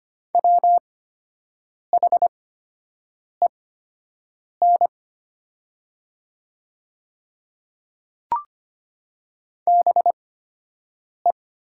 Morse code, heard as a keyed tone, about 25 wpm, spaced as for 7 wpm, 700 Hz. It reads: WHEN BE
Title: 8x Character Spacing - Top 300 Words - 25wpm